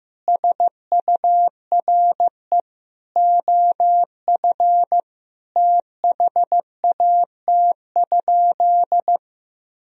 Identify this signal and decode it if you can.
Morse code, heard as a keyed tone, about 15 wpm, 700 Hz. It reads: SURE OF THAT?
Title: Sentences from Top 300 Words and Abbreviations 15wpm